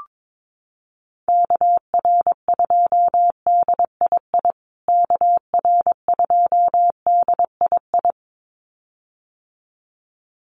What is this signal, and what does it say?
Morse code, heard as a keyed tone, about 22 wpm, 700 Hz. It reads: KR2DII KR2DII
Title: USA Call Signs Repeated 2x 22wpm